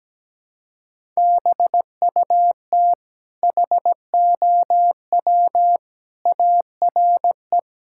Morse code, heard as a keyed tone, about 17 wpm, 700 Hz. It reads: BUT HOW ARE
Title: Sentences from Top 1000 Words - Review Part 6 - 17wpm